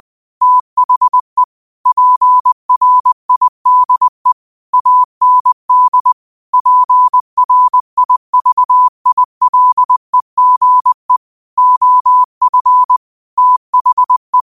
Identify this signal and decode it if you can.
Morse code, heard as a keyed tone, about 20 wpm, 1000 Hz. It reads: THE PRIDE AND PRIVILEGE OF THE